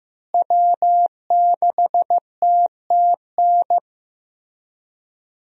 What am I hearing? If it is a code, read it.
Morse code, heard as a keyed tone, about 15 wpm, 700 Hz. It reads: W6TTN
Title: USA Call Signs - 5 Characters - Repeated 2x - 15wpm